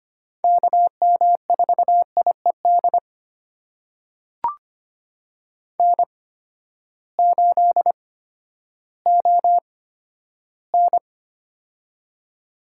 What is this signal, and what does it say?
Morse code, heard as a keyed tone, about 25 wpm, spaced as for 12 wpm, 700 Hz. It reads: KM4IED N8ON
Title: USA Call Signs - 8x Character Spacing - 25wpm